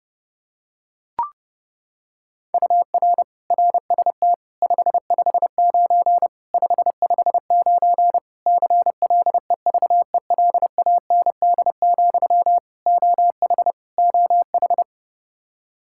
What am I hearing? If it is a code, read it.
Morse code, heard as a keyed tone, about 32 wpm, 700 Hz. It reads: UR RST 559 559 CLEVELAND, OH OH
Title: Practice QSO - Exchange 1 - Beginning - 30wpm